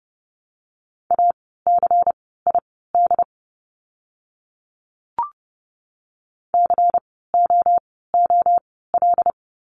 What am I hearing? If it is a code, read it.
Morse code, heard as a keyed tone, about 30 wpm, spaced as for 17 wpm, 700 Hz. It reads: ACID COOL
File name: Lesson 16f - Letter c - Words 30wpm